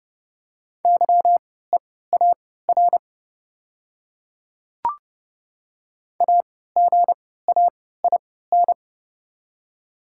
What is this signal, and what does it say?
Morse code, heard as a keyed tone, about 30 wpm, spaced as for 17 wpm, 700 Hz. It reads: YEAR AGAIN
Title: 3x Character Spacing - Top 200 Words - 30wpm